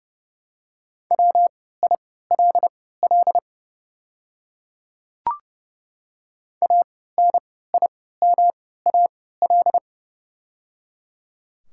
Morse code, heard as a keyed tone, about 30 wpm, spaced as for 17 wpm, 700 Hz. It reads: WILL ANIMAL
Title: 3x Character Spacing - Top 200 Words - 30wpm